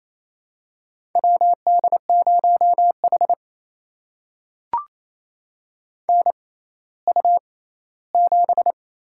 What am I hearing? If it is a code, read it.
Morse code, heard as a keyed tone, about 28 wpm, spaced as for 16 wpm, 700 Hz. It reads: WD0H NU7
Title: USA Call Signs - 4 Characters - 6x Character Spacing - 28wpm